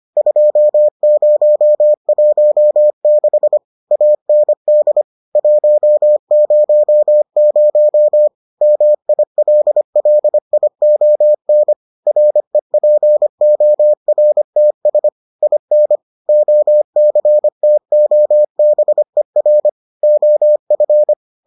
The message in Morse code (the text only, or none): 2016 AND 100 MILLION REPORTS IN OCTOBER OF